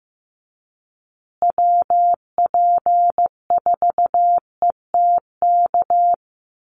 Morse code, heard as a keyed tone, about 15 wpm, 700 Hz. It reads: WP4ETK
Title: USA Call Signs Repeated 2x 15wpm